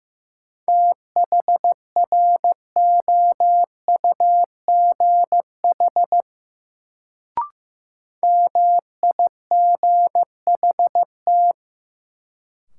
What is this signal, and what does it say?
Morse code, heard as a keyed tone, about 15 wpm, 700 Hz. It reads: THROUGH MIGHT
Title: Sets of 2 Words - New Word Drill - 15wpm